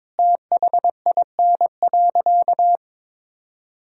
Morse code, heard as a keyed tone, about 22 wpm, 700 Hz. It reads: THIN.
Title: Sentences from Top 1600 Words 22wpm